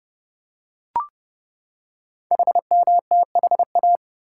Morse code, heard as a keyed tone, about 28 wpm, 700 Hz. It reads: HMTHA
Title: Code Groups - 5 Letters - 30wpm